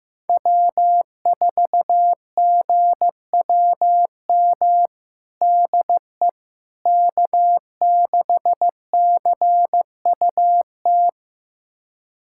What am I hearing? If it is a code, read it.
Morse code, heard as a keyed tone, about 15 wpm, 700 Hz. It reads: W4GWM DE K6CUT
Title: USA Callsign DE Callsign - 5 Characters - 15wpm